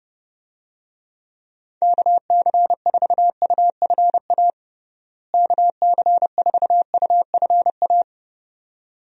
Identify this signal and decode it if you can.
Morse code, heard as a keyed tone, about 30 wpm, 700 Hz. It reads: KC4UFA KC4UFA
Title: USA Call Signs - 6 Characters - Repeated 2x - 30wpm